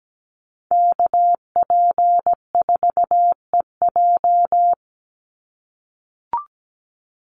Morse code, heard as a keyed tone, about 17 wpm, 700 Hz. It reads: KP4EJ
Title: Call Signs - 17wpm